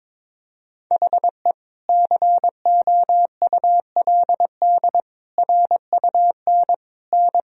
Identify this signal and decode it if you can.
Morse code, heard as a keyed tone, about 22 wpm, 700 Hz. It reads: HE COULD RUN N
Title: Sentences from Top 300 Words and Abbreviations 22wpm